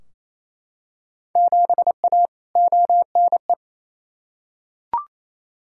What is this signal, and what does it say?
Morse code, heard as a keyed tone, about 28 wpm, 700 Hz. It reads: 7A ONE
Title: Field Day Exchanges Encore - 28wpm